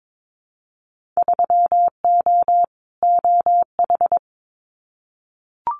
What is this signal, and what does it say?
Morse code, heard as a keyed tone, about 22 wpm, 700 Hz. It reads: 3O OH
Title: Winter Field Day Exchanges 22wpm